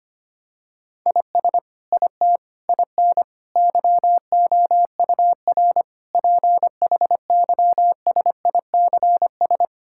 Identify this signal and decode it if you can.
Morse code, heard as a keyed tone, about 25 wpm, 700 Hz. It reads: IS IT IN YOUR PHYSICS
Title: Sentences from Top 1000 Words - Review Part 5 - 25wpm